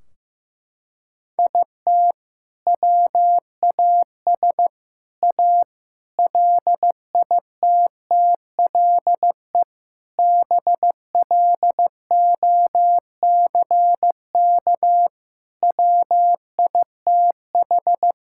Morse code, heard as a keyed tone, about 15 wpm, 700 Hz. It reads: IT WAS A LITTLE BLOCK WITH
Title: Sentences from Top 1600 Words 15wpm